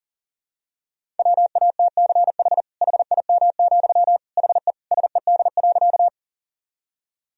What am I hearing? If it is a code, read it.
Morse code, heard as a keyed tone, about 40 wpm, 700 Hz. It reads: WATCH HIM, HE SED.
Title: Sentences from Top 300 Words and Abbreviations 40wpm